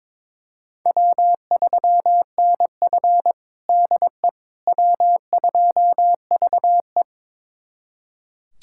Morse code, heard as a keyed tone, about 22 wpm, 700 Hz. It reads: W3NF DE W2VE
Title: USA Callsign DE Callsign - 4 Characters - 22wpm